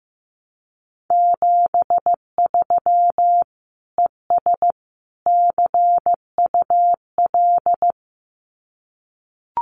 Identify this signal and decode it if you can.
Morse code, heard as a keyed tone, about 15 wpm, 700 Hz. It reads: 73 ES CUL
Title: Common QSO Phrases Encore 15wpm